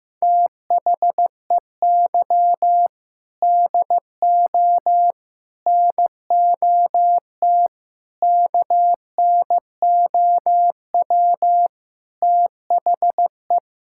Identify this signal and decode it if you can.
Morse code, heard as a keyed tone, about 15 wpm, 700 Hz. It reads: THEY DO NOT KNOW THE
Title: Sentences from Top 1000 Words - Review Part 6 - 15wpm